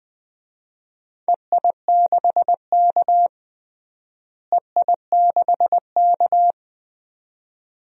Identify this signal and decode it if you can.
Morse code, heard as a keyed tone, about 20 wpm, 700 Hz. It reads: EI6K EI6K
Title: DXCC-100 Call Signs - 4 Characters - Repeated 2x - 20wpm